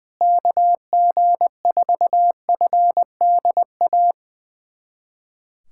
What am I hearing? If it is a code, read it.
Morse code, heard as a keyed tone, about 20 wpm, 700 Hz. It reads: KG4FDA
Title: USA Callsign DE Callsign - 6 Characters - 20wpm